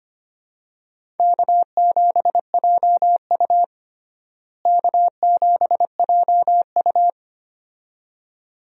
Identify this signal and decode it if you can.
Morse code, heard as a keyed tone, about 25 wpm, 700 Hz. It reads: K7JU K7JU
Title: USA Call Signs - 4 Characters - Repeated 2x - 25wpm